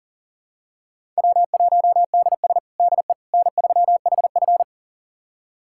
Morse code, heard as a keyed tone, about 37 wpm, 700 Hz. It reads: W1DS DE N3HF
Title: USA Callsign DE Callsign - 4 Characters - 40wpm